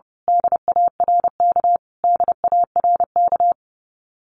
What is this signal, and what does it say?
Morse code, heard as a keyed tone, about 30 wpm, 700 Hz. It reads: DARK DARK
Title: Top 500 Words Repeated 30wpm